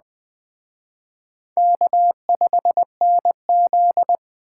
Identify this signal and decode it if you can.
Morse code, heard as a keyed tone, about 20 wpm, 700 Hz. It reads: K5NZ
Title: Speed Racing - USA 1x2 Call Signs 17wpm